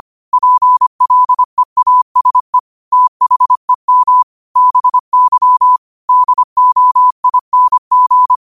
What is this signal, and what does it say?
Morse code, heard as a keyed tone, about 25 wpm, 1000 Hz. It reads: PLEASE THEM BY DOING